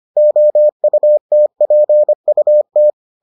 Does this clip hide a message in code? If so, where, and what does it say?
Yes: Morse code from 0.2 to 2.9 s, heard as a keyed tone, about 25 wpm, 600 Hz. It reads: OUTPUT